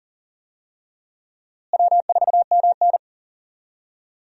Morse code, heard as a keyed tone, about 40 wpm, 700 Hz. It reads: W4MN
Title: USA Call Signs - 4 Characters - Repeated 3x - 40wpm